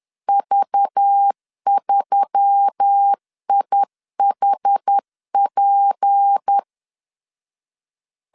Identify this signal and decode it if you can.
Morse code, heard as a keyed tone, about 10 wpm, 800 Hz. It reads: V3IHP